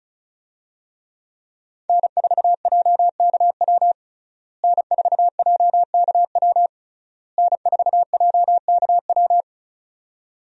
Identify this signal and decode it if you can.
Morse code, heard as a keyed tone, about 36 wpm, 700 Hz. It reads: N4JKW N4JKW N4JKW
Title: USA Call Signs - 5 Characters - Repeated 3x - 35wpm